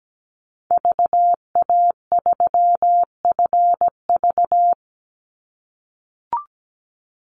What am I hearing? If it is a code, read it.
Morse code, heard as a keyed tone, about 17 wpm, 700 Hz. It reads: VA3FV
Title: Call Signs - 17wpm